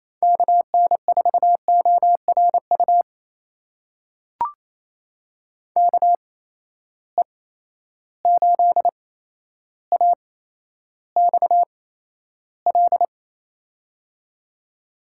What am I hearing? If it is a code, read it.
Morse code, heard as a keyed tone, about 28 wpm, spaced as for 12 wpm, 700 Hz. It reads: KN4ORU KE8AXL
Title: USA Call Signs - 6 Characters - 8x Character Spacing - 28wpm